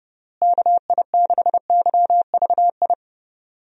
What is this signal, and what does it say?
Morse code, heard as a keyed tone, about 30 wpm, 700 Hz. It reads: KI6YVI
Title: USA Call Signs - 6 Characters - Repeated 2x - 30wpm